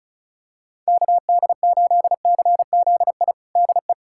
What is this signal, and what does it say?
Morse code, heard as a keyed tone, about 36 wpm, 700 Hz. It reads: KD8CZI DE
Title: USA Callsign DE Callsign 35wpm